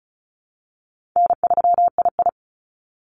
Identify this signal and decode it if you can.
Morse code, heard as a keyed tone, about 36 wpm, 700 Hz. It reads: N3II
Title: NA SOTA Activator Callsigns 35wpm